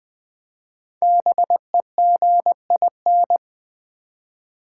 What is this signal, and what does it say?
Morse code, heard as a keyed tone, about 20 wpm, 700 Hz. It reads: BEGIN
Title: Five Letter Words - 20wpm